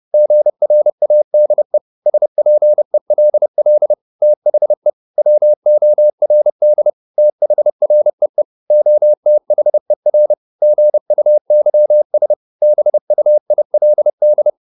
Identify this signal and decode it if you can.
Morse code, heard as a keyed tone, about 30 wpm, 600 Hz. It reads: GRADE SPELL THE WORD THREE OTHER GUYS BUILD